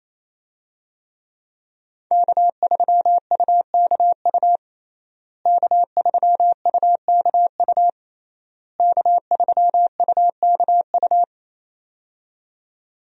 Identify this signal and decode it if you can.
Morse code, heard as a keyed tone, about 28 wpm, 700 Hz. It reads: K3UKU K3UKU K3UKU
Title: USA Call Signs - 5 Characters - Repeated 3x - 28wpm